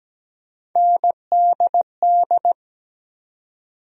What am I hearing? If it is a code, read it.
Morse code, heard as a keyed tone, about 17 wpm, 700 Hz. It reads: NDD
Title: USA Call Sign Postfixes - 3 Letters 17wpm